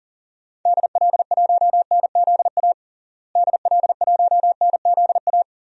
Morse code, heard as a keyed tone, about 40 wpm, 700 Hz. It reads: DL1NZA DL1NZA
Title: DXCC-100 Call Signs - 6 Characters - Repeated 3x - 40wpm